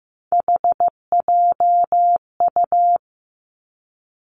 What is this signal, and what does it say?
Morse code, heard as a keyed tone, about 15 wpm, 700 Hz. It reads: HJU
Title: Three Challenging Letter-Number Combinations 15wpm